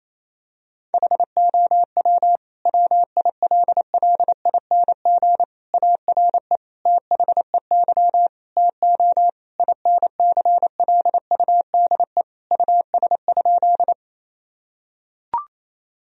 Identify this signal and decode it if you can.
Morse code, heard as a keyed tone, about 28 wpm, 700 Hz. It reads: HOW WILLING ARE THEY TO INCLUDE US?
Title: Sentences from Top 1000 Words - Review Part 3 - 28wpm